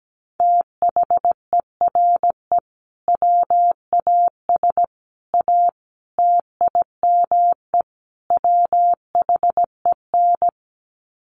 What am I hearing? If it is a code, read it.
Morse code, heard as a keyed tone, about 17 wpm, 700 Hz. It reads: THERE WAS A TIME WHEN
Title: Sets of 5 Words 17wpm